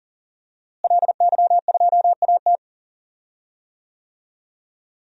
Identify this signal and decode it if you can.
Morse code, heard as a keyed tone, about 40 wpm, 700 Hz. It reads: LY2AT
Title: DXCC-100 Call Signs - 5 Characters - Repeated 2x - 40wpm